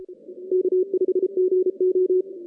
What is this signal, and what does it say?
Morse code, heard as a keyed tone, about 32 wpm, 370 Hz. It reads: K5GO
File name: 2016 JAN CW NAQP / 2016 Jan naqp k5ot - 0-9